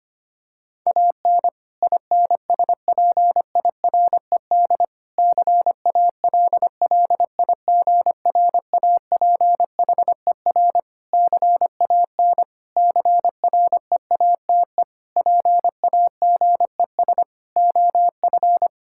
Morse code, heard as a keyed tone, about 25 wpm, 700 Hz. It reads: AN INSPIRED CALLIGRAPHER CAN CREATE PAGES OF
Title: Pangram Challenge v2 - 25wpm